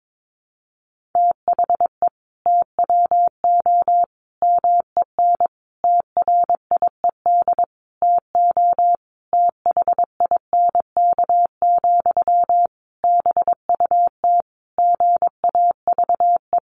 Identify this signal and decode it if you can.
Morse code, heard as a keyed tone, about 22 wpm, 700 Hz. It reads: THE TWO MEN TRIED TO THINK, BUT GAVE
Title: A Sound Of Thunder 22wpm